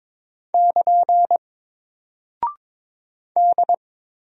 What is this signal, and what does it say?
Morse code, heard as a keyed tone, about 22 wpm, 700 Hz. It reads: <KN> D
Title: Single Character (Letter:Number:Punctuation:Prosign) 22wpm